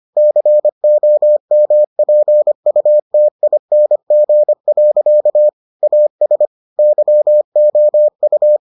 Morse code, heard as a keyed tone, about 25 wpm, 600 Hz. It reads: COMPUTING. AS YOU